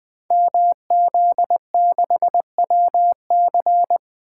Morse code, heard as a keyed tone, about 20 wpm, 700 Hz. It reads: MZ6WC
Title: Code Groups - 5 Letter:Numbers - 20wpm